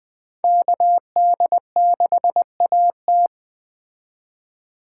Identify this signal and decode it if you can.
Morse code, heard as a keyed tone, about 21 wpm, 700 Hz. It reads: KD6AT
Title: Speed Racing - USA 2x2 Call Signs 17wpm